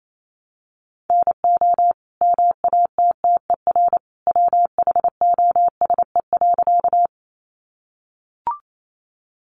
Morse code, heard as a keyed tone, about 28 wpm, 700 Hz. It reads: NO MATTER WHOSE.
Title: Sentences from Top 1200 Words 28wpm